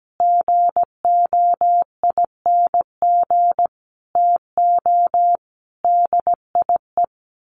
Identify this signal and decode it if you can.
Morse code, heard as a keyed tone, about 17 wpm, 700 Hz. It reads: GOING TO DIE
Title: A Sound Of Thunder 17wpm